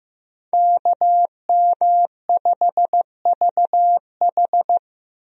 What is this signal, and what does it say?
Morse code, heard as a keyed tone, about 15 wpm, 700 Hz. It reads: KM5VH
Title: USA Call Signs - 5 Characters - 15wpm